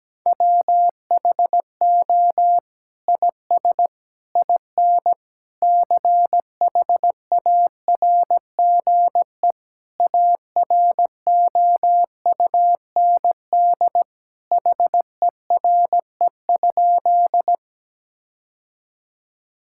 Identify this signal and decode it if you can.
Morse code, heard as a keyed tone, about 17 wpm, 700 Hz. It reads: WHO IS IN CHARGE AROUND HERE?
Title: Sentences from Top 1000 Words - Review Part 6 - 17wpm